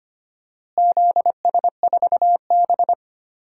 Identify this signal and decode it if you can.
Morse code, heard as a keyed tone, about 25 wpm, 700 Hz. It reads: ZS4B
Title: DXCC-100 Call Signs - 4 Characters - Repeated 2x - 25wpm